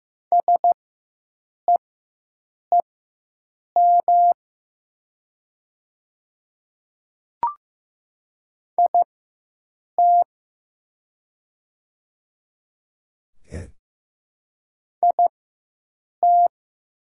Morse code, heard as a keyed tone, about 15 wpm, spaced as for 7 wpm, 700 Hz. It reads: SEEM IT IT
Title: 4x Character Spacing - Top 300 Words - 15wpm